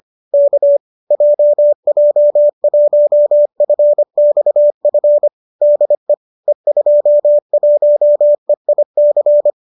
Morse code, heard as a keyed tone, about 25 wpm, 590 Hz. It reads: K JJ1FXF DE E21EIC
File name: qso208 25